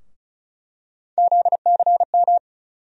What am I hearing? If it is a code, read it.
Morse code, heard as a keyed tone, about 34 wpm, 700 Hz. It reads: ZCM
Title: USA Call Sign Postfixes - 3 Letters 35wpm